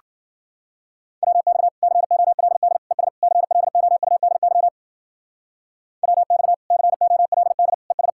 Morse code, heard as a keyed tone, about 47 wpm, 700 Hz. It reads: WX COLD ES CLOUDY WX COLD ES
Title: Speed-Racing Common QSO Phrases - 40wpm